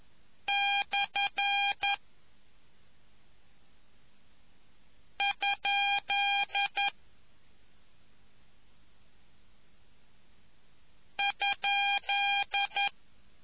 Morse code, heard as a keyed tone, about 11 wpm, spaced as for 2 wpm, 2400 Hz. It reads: /??